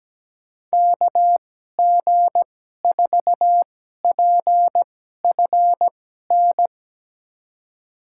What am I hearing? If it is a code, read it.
Morse code, heard as a keyed tone, about 17 wpm, spaced as for 12 wpm, 700 Hz. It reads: KG4PFN
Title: USA Call Signs - 2x Character Spacing - 17wpm